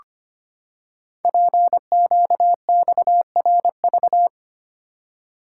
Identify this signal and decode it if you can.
Morse code, heard as a keyed tone, about 25 wpm, 700 Hz. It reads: PQXRV
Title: Code Groups - 5 Letters - 25wpm